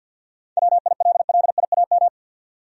Morse code, heard as a keyed tone, about 52 wpm, 700 Hz. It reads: WILLIAM
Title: USA Common Names - 50wpm